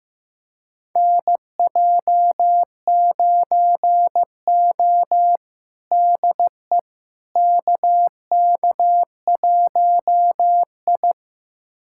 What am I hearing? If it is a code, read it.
Morse code, heard as a keyed tone, about 16 wpm, 700 Hz. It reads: NJ9O DE KK1I
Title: USA Callsign DE Callsign - 4 Characters - 15wpm